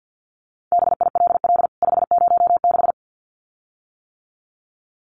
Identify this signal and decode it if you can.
Morse code, heard as a keyed tone, about 50 wpm, 700 Hz. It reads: BILL 506
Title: CWT Exchange 50wpm